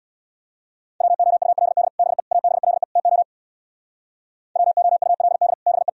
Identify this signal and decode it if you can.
Morse code, heard as a keyed tone, about 64 wpm, 700 Hz. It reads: WOULD BE ABLE TO WOULD BE